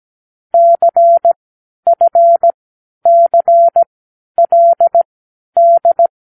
Morse code, heard as a keyed tone, about 17 wpm, spaced as for 10 wpm, 680 Hz. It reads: CFCLD
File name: k7qo / Lesson18(A-L)